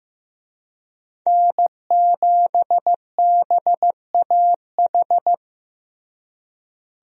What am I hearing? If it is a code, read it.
Morse code, heard as a keyed tone, about 15 wpm, 700 Hz. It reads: N7BAH
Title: USA Call Signs - 5 Characters - Repeated 2x - 15wpm